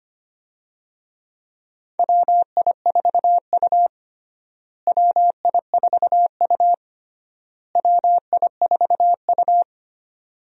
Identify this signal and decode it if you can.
Morse code, heard as a keyed tone, about 25 wpm, 700 Hz. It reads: WI4U WI4U WI4U
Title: USA Call Signs - 4 Characters - Repeated 3x - 25wpm